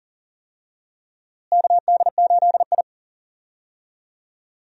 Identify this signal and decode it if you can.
Morse code, heard as a keyed tone, about 40 wpm, 700 Hz. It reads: KD8I
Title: USA Call Signs - 4 Characters - Repeated 2x - 40wpm